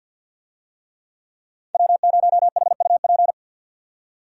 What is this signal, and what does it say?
Morse code, heard as a keyed tone, about 52 wpm, 700 Hz. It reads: W0HUP
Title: USA Call Signs - 5 Characters - 50wpm